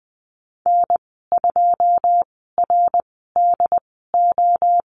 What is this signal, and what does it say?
Morse code, heard as a keyed tone, about 20 wpm, spaced as for 14 wpm, 700 Hz. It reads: N2RDO
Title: 207 Lesson 33g - Letter j - Call Signs 20wpm